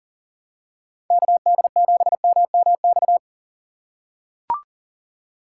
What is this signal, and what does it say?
Morse code, heard as a keyed tone, about 40 wpm, 700 Hz. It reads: KD7MMX
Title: USA Call Signs - 6 Characters - 4x Character Spacing - 40wpm